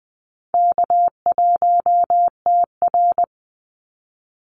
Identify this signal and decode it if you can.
Morse code, heard as a keyed tone, about 19 wpm, 700 Hz. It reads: K1TR
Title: Easy Call Signs - 20wpm